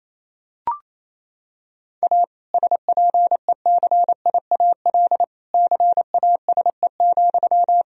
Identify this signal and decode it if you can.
Morse code, heard as a keyed tone, about 28 wpm, 700 Hz. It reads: A SPECIAL CASE,
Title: Sentences from Top 1000 Words - Review Part 3 - 28wpm